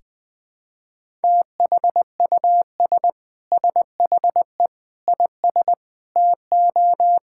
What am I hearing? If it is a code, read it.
Morse code, heard as a keyed tone, about 20 wpm, 700 Hz. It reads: THUS SHE IS TO
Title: Sentences from Top 1000 Words - Review Part 5 - 20wpm